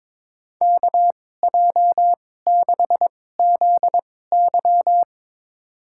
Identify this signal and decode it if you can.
Morse code, heard as a keyed tone, about 22 wpm, spaced as for 16 wpm, 700 Hz. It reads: KJ6ZY
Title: USA Call Signs - 5 Characters - 2x Character Spacing - 22wpm